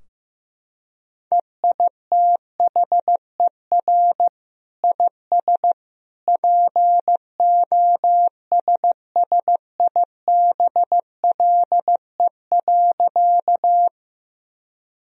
Morse code, heard as a keyed tone, about 15 wpm, 700 Hz. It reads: EITHER IS POSSIBLE.
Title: Sentences from Top 1000 Words - Review Part 4 - 15wpm